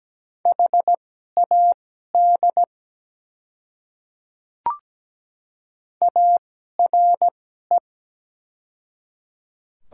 Morse code, heard as a keyed tone, about 17 wpm, spaced as for 12 wpm, 700 Hz. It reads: HAD ARE
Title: 2x Character Spacing - Top 100 Words - 17wpm